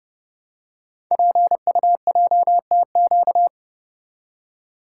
Morse code, heard as a keyed tone, about 30 wpm, 700 Hz. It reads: PUJTQ
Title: Code Groups - 5 Letter:Numbers - 30wpm